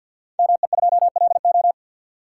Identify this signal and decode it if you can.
Morse code, heard as a keyed tone, about 52 wpm, 700 Hz. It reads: ME2LO
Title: Code Groups - 5 Letter:Numbers - 50wpm